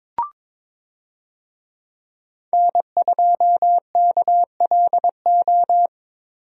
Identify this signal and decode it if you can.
Morse code, heard as a keyed tone, about 22 wpm, 700 Hz. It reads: N2KLO
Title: USA Call Signs - 5 Characters - Repeated 3x - 22wpm